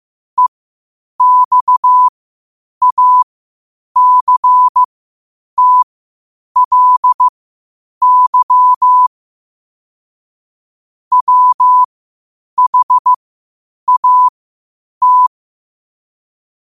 Morse code, heard as a keyed tone, about 15 wpm, spaced as for 8 wpm, 1000 Hz. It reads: EXACTLY WHAT
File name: qotdfiles / 8wpm